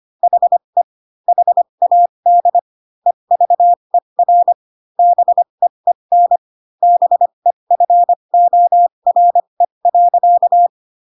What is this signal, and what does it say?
Morse code, heard as a keyed tone, about 25 wpm, spaced as for 22 wpm, 700 Hz. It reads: HE HAD EVER BEEN BEFORE.
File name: Pinocchio / Pinocchio ch3 18 25 ik1hge